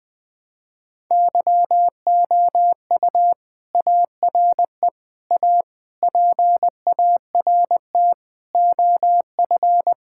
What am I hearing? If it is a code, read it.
Morse code, heard as a keyed tone, about 21 wpm, 700 Hz. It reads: YOU ARE A PART OF
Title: Sentences with 3-Word Phrases - Spoken Beginning - 20wpm